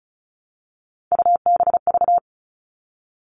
Morse code, heard as a keyed tone, about 34 wpm, 700 Hz. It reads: UBV
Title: All Cut Numbers - 3-Digit 35wpm